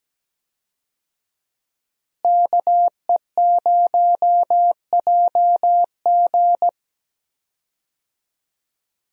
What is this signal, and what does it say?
Morse code, heard as a keyed tone, about 17 wpm, 700 Hz. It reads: KE0JG
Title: USA Call Signs - 5 Characters - Repeated 3x - 17wpm